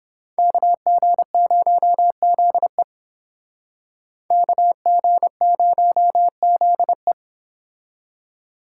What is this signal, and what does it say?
Morse code, heard as a keyed tone, about 31 wpm, 700 Hz. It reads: KG0ZE KG0ZE